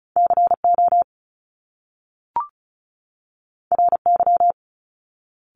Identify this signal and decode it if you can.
Morse code, heard as a keyed tone, about 36 wpm, 700 Hz. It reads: CO RY
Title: Top Two Letter Combinations in Words 35wpm